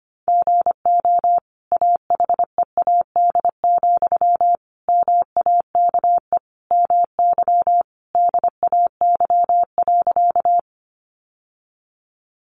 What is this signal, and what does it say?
Morse code, heard as a keyed tone, about 25 wpm, 700 Hz. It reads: GO AHEAD, MAKE MY DAY.
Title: Sentences from Top 1100 Words 25wpm